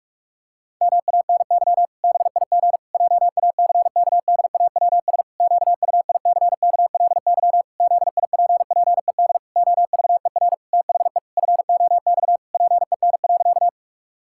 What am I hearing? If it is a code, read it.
Morse code, heard as a keyed tone, about 46 wpm, 700 Hz. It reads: MANY BIG JACKDAWS QUICKLY ZIPPED OVER THE FOX PEN.
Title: Pangram Challenge v2 - 45wpm